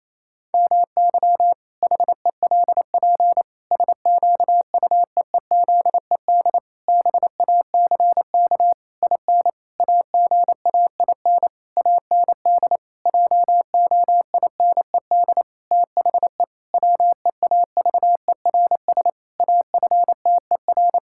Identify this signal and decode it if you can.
Morse code, heard as a keyed tone, about 28 wpm, 700 Hz. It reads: MY HELP SQUEEZED BACK IN AGAIN AND JOINED THE WEAVERS AFTER
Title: Pangram Challenge v2 - 28wpm